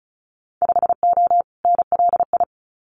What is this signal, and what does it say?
Morse code, heard as a keyed tone, about 34 wpm, 700 Hz. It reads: 5O NLI
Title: Winter Field Day Exchanges 35wpm